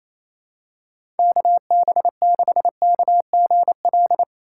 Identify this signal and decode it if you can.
Morse code, heard as a keyed tone, about 28 wpm, 700 Hz. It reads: KB6KGL
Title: USA Call Signs - 6 Characters - 2x Character Spacing - 28wpm